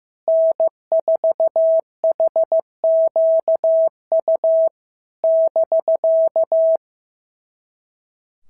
Morse code, heard as a keyed tone, about 14 wpm, 650 Hz. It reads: N4HQU <BK>
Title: SOTA QSO Practice 15wpm